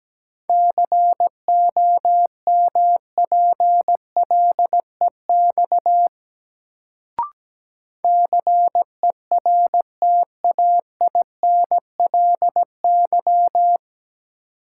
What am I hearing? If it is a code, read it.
Morse code, heard as a keyed tone, about 17 wpm, 700 Hz. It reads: COMPLEX CERTAINLY
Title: Sentences from Top 900 Words - New Word Drill - 17wpm